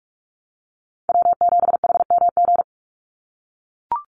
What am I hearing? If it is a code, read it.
Morse code, heard as a keyed tone, about 44 wpm, 700 Hz. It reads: W7HMG
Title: USA Call Signs 45wpm